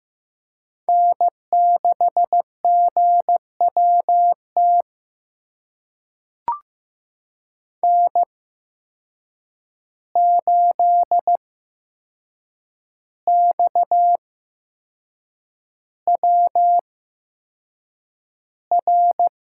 Morse code, heard as a keyed tone, about 15 wpm, spaced as for 6 wpm, 700 Hz. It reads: N6GWT N8XWR
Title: USA Call Signs - 5 Characters - 8x Character Spacing - 15wpm